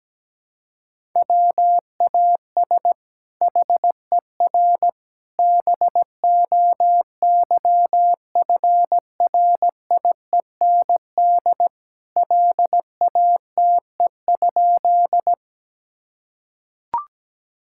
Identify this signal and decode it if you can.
Morse code, heard as a keyed tone, about 17 wpm, 700 Hz. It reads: WAS HER BOYFRIEND LATE?
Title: Sentences from Top 1000 Words - Review Part 5 - 17wpm